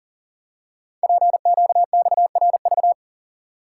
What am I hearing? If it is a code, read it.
Morse code, heard as a keyed tone, about 42 wpm, 700 Hz. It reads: PQXRV
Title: Code Groups - 5 Letters - 40wpm